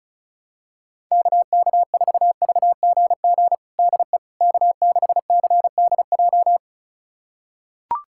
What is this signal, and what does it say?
Morse code, heard as a keyed tone, about 36 wpm, 700 Hz. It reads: KK4VGG DE K6CDJ
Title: USA Callsign DE Callsign 35wpm